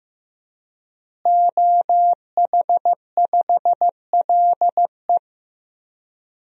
Morse code, heard as a keyed tone, about 16 wpm, 700 Hz. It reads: OH5LE
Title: DXCC-100 Call Signs - 5 Characters - Repeated 2x - 15wpm